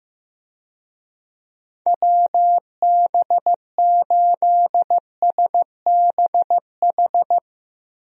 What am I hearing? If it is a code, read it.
Morse code, heard as a keyed tone, about 15 wpm, 700 Hz. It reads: WB8SBH
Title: USA Call Signs - 6 Characters - Repeated 3x - 15wpm